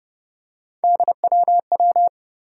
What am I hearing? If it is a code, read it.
Morse code, heard as a keyed tone, about 30 wpm, 700 Hz. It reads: DWW
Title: USA Call Sign Postfixes - 3 Letters 30wpm